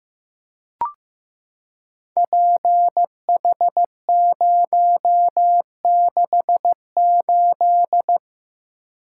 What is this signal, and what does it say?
Morse code, heard as a keyed tone, about 15 wpm, 700 Hz. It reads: PH068
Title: Code Groups - 5 Letter:Numbers - 15wpm